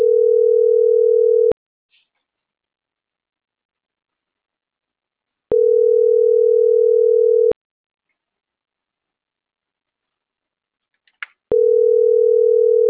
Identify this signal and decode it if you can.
Ringback tone (North America), 440 + 480 Hz.